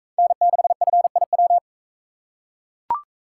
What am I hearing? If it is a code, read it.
Morse code, heard as a keyed tone, about 40 wpm, 700 Hz. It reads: N6FIW